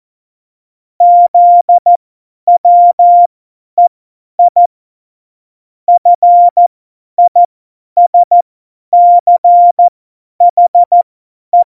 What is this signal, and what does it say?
Morse code, heard as a keyed tone, about 14 wpm, spaced as for 10 wpm, 700 Hz. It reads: ZWEI FISCHE
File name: Morsekurs / Witze / Witz 31.Juli 2018, 50 F70